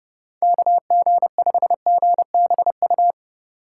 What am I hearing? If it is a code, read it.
Morse code, heard as a keyed tone, about 31 wpm, 700 Hz. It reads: KG5GBU
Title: USA Call Signs - 6 Characters - 8x Character Spacing - 30wpm